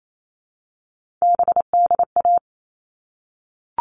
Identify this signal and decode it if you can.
Morse code, heard as a keyed tone, about 28 wpm, 700 Hz. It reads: BDA
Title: All Cut Numbers - 3-Digit 28wpm